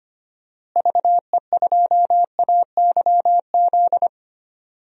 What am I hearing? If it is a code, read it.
Morse code, heard as a keyed tone, about 25 wpm, 700 Hz. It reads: VE2AYZ
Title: DXCC-100 Call Signs - 6 Characters - Repeated 3x - 25wpm